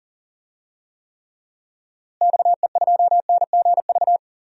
Morse code, heard as a keyed tone, about 40 wpm, 700 Hz. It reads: XE2NGV
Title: DXCC-100 Call Signs - 6 Characters - Repeated 3x - 40wpm